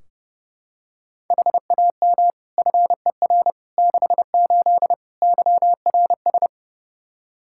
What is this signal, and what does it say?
Morse code, heard as a keyed tone, about 32 wpm, 700 Hz. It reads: HAM FER 68 YRS
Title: K1USN SST QSO Practice 30wpm